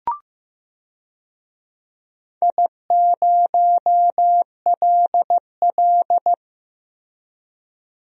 Morse code, heard as a keyed tone, about 15 wpm, 700 Hz. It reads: I0LL
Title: DXCC-100 Call Signs - 4 Characters - Repeated 2x - 15wpm